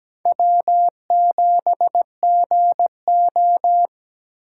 Morse code, heard as a keyed tone, about 17 wpm, 700 Hz. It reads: W7GO
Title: USA Call Signs - 4 Characters - 4x Character Spacing - 17wpm